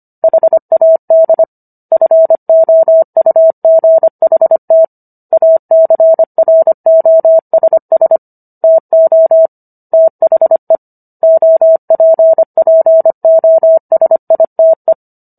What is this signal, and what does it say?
Morse code, heard as a keyed tone, about 25 wpm, 650 Hz. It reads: HAD FOUGHT ACROSS TO THE OPPOSITE